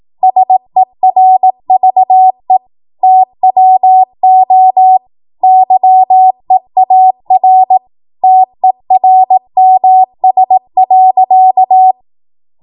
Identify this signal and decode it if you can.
Morse code, heard as a keyed tone, about 18 wpm, 750 Hz. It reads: SERVE TWO YEAR TERMS.
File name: Morse / 18WPM